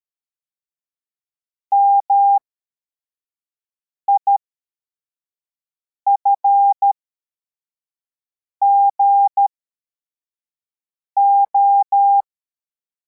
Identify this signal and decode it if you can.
Morse code, heard as a keyed tone, about 13 wpm, spaced as for 4 wpm, 800 Hz. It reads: MIFGO
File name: formation / cw / Text64